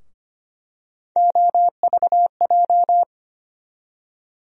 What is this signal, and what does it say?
Morse code, heard as a keyed tone, about 25 wpm, 700 Hz. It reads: OVJ
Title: USA Call Sign Postfixes - 3 Letters 25wpm